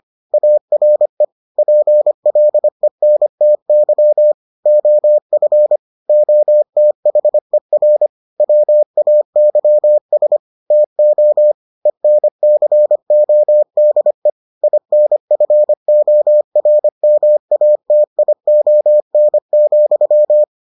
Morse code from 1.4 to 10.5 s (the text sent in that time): PLENTY OF OTHER WAYS